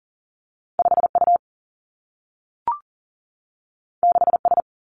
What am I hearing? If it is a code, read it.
Morse code, heard as a keyed tone, about 38 wpm, 700 Hz. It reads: 5 U 6 S
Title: Two Challenging Letter-Number Combinations 40wpm